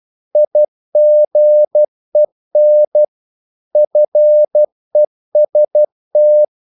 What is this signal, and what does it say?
Morse code, heard as a keyed tone, about 12 wpm, 600 Hz. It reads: IGEN FEST
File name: Korte ord DK 12wpm